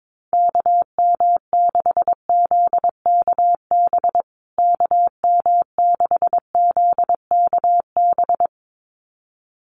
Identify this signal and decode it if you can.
Morse code, heard as a keyed tone, about 22 wpm, 700 Hz. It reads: KM6ZKB KM6ZKB
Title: USA Call Signs Repeated 2x 22wpm